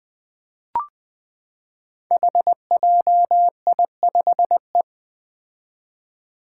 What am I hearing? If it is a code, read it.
Morse code, heard as a keyed tone, about 20 wpm, 700 Hz. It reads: HJI5E
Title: Five Challenging Letter-Number Combinations 20wpm